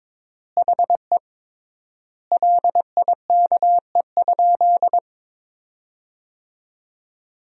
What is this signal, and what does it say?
Morse code, heard as a keyed tone, about 22 wpm, 700 Hz. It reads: HE LIKE?
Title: 3x Word-Spacing - Sentences from Top 100 Words - 22wpm